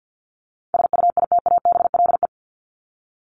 Morse code, heard as a keyed tone, about 50 wpm, 700 Hz. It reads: SUITABLE
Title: Top 1251-1500 Words 50wpm